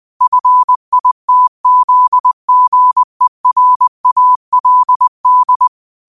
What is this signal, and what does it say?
Morse code, heard as a keyed tone, about 20 wpm, 1000 Hz. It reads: FITZGERALD